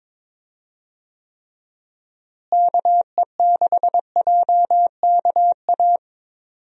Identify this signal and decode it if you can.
Morse code, heard as a keyed tone, about 22 wpm, 700 Hz. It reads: KE6JKA
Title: USA Call Signs - 6 Characters - Repeated 2x - 22wpm